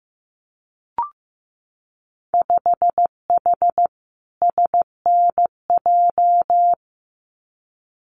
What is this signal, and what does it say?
Morse code, heard as a keyed tone, about 15 wpm, 700 Hz. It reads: 5H SNJ
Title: Winter Field Day Exchanges Encore 15wpm